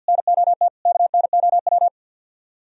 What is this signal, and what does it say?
Morse code, heard as a keyed tone, about 65 wpm, spaced as for 57 wpm, 700 Hz. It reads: NOT KNOW
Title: Speed-Racing - Sets of 4 Words 50wpm